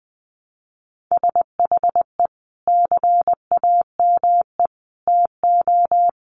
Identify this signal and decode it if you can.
Morse code, heard as a keyed tone, about 20 wpm, 700 Hz. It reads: SHE CAME TO
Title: Sentences from Top 1100 Words 20wpm